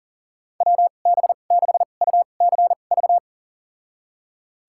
Morse code, heard as a keyed tone, about 40 wpm, spaced as for 29 wpm, 700 Hz. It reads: WB6UCV
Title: USA Call Signs - 2x Character Spacing - 40wpm